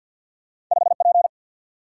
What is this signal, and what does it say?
Morse code, heard as a keyed tone, about 49 wpm, 700 Hz. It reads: 5 P
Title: Single-Letter Single-Number 50wpm